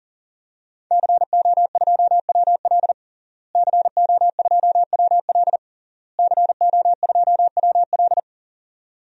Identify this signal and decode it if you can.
Morse code, heard as a keyed tone, about 40 wpm, 700 Hz. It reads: CO2WL CO2WL CO2WL
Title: DXCC-100 Call Signs - 5 Characters - Repeated 3x - 40wpm